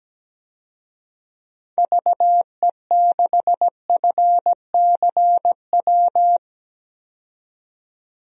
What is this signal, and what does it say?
Morse code, heard as a keyed tone, about 17 wpm, 700 Hz. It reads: VE6FCW
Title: DXCC-100 Call Signs - 6 Characters - 17wpm